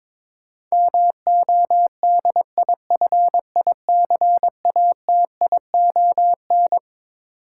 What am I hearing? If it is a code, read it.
Morse code, heard as a keyed tone, about 22 wpm, 700 Hz. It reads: MODIFICATION
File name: Twelve Letter Words - 22wpm